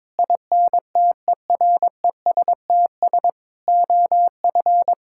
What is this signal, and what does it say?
Morse code, heard as a keyed tone, about 22 wpm, 700 Hz. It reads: INTERESTS OF
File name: practice / 22wpm / Sets of 6 Words - 22wpm